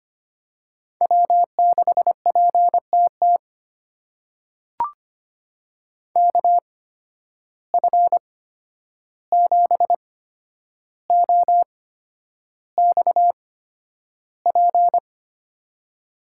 Morse code, heard as a keyed tone, about 25 wpm, spaced as for 10 wpm, 700 Hz. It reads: W6PTT KF7OXP
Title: USA Call Signs - 8x Character Spacing - 25wpm